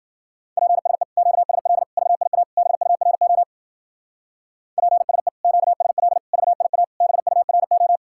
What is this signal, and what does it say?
Morse code, heard as a keyed tone, about 51 wpm, 700 Hz. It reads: PSE QSL VIA BURO PSE QSL VIA BURO